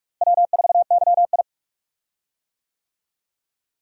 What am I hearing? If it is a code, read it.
Morse code, heard as a keyed tone, about 46 wpm, 700 Hz. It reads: W4YI
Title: USA Call Signs - 4 Characters - Repeated 2x - 45wpm